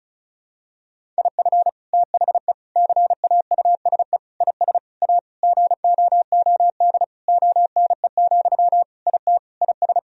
Morse code, heard as a keyed tone, about 36 wpm, 700 Hz. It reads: IF THE CAUSE IS A GOOD ONE, IT IS
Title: Sentences from Top 1000 Words - Review Part 1 - 35wpm